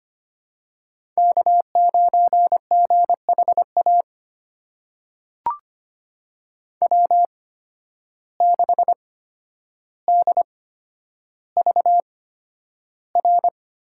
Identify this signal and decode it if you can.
Morse code, heard as a keyed tone, about 25 wpm, spaced as for 11 wpm, 700 Hz. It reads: K9GHA W6DVR
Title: USA Call Signs - 5 Characters - 8x Character Spacing - 25wpm